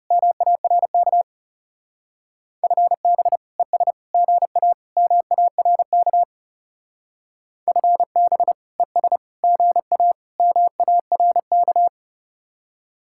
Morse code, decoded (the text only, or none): MARK FB ES GA MARK FB ES GA MARK